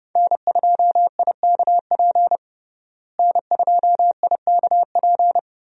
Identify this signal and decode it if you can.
Morse code, heard as a keyed tone, about 31 wpm, 700 Hz. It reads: N2IKP N2IKP
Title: USA Call Signs - 5 Characters - Repeated 3x - 30wpm